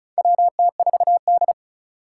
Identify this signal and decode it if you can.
Morse code, heard as a keyed tone, about 36 wpm, 700 Hz. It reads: WT4D
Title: USA Call Signs - 4 Characters - 4x Character Spacing - 35wpm